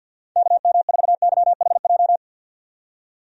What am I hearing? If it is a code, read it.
Morse code, heard as a keyed tone, about 50 wpm, 700 Hz. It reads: KM4YHJ
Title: USA Call Signs - 6 Characters - 50wpm